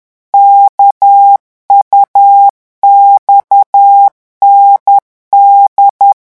Morse code, heard as a keyed tone, about 10 wpm, 800 Hz. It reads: KUXND
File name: morse mp3 / revision3-vitesse 10 mots